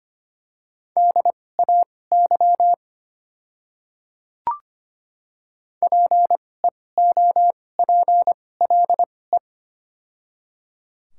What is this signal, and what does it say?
Morse code, heard as a keyed tone, about 25 wpm, spaced as for 18 wpm, 690 Hz. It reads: DAY PEOPLE
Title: 2x Character Spacing - Top 100 Words - 25wpm